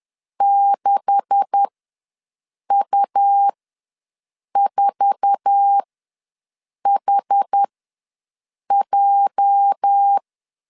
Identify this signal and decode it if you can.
Morse code, heard as a keyed tone, about 10 wpm, spaced as for 6 wpm, 800 Hz. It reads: 6U4HJ
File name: LEZIONI / 9 286 ws 10wpm 4032